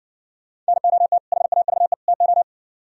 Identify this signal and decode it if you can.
Morse code, heard as a keyed tone, about 61 wpm, 700 Hz. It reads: NOT HAVE TO